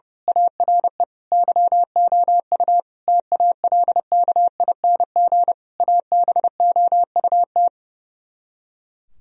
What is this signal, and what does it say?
Morse code, heard as a keyed tone, about 28 wpm, 700 Hz. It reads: ARE YOU TALKING ABOUT
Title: Speed-Racing - Sets of 5 Words 30wpm